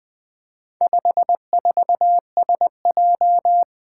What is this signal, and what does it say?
Morse code, heard as a keyed tone, about 20 wpm, 700 Hz. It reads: 54SJ
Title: Four Challenging Letter-Number Combinations 20wpm